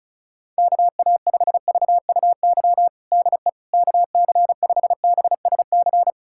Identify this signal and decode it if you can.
Morse code, heard as a keyed tone, about 34 wpm, 690 Hz. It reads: KA5VUY DE KC5BSC
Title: USA Callsign DE Callsign - 6 Characters - 35wpm